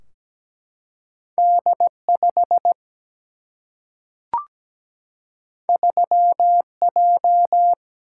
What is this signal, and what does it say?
Morse code, heard as a keyed tone, about 17 wpm, 700 Hz. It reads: D 5 3 J
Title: Single-Letter Single-Number 17wpm